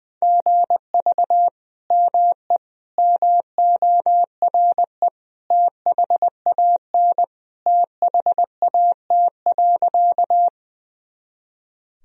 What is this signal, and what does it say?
Morse code, heard as a keyed tone, about 20 wpm, 700 Hz. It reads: GV ME MORE THAN THAT.
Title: Sentences from Top 200 Words and Abbreviations 20wpm